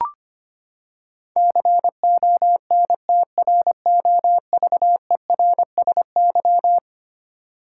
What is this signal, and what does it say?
Morse code, heard as a keyed tone, about 25 wpm, 700 Hz. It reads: CONTROVERSY
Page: morse-code-ninja-practice-25wpm